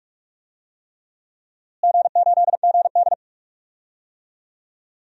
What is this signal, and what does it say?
Morse code, heard as a keyed tone, about 45 wpm, 700 Hz. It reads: G8GD
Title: DXCC-100 Call Signs - 4 Characters - Repeated 3x - 45wpm